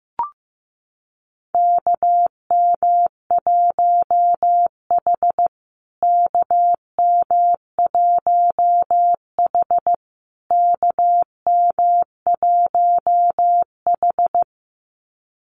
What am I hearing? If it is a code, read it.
Morse code, heard as a keyed tone, about 15 wpm, 700 Hz. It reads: KM1H KM1H KM1H
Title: USA Call Signs Repeated 3x 15wpm